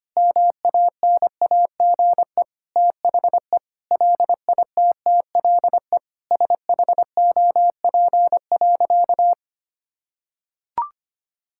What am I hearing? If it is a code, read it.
Morse code, heard as a keyed tone, about 25 wpm, 700 Hz. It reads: MANAGE THE LITTLE SHOP.
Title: Sentences from Top 1600 Words 25wpm